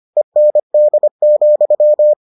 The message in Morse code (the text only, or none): END,